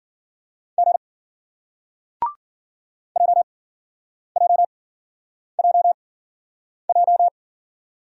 Morse code, heard as a keyed tone, about 55 wpm, spaced as for 9 wpm, 700 Hz. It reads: G JJJJ